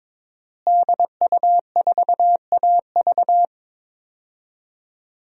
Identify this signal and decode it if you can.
Morse code, heard as a keyed tone, about 22 wpm, 700 Hz. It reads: DU4AV
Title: DXCC-100 Call Signs - 5 Characters - Repeated 2x - 22wpm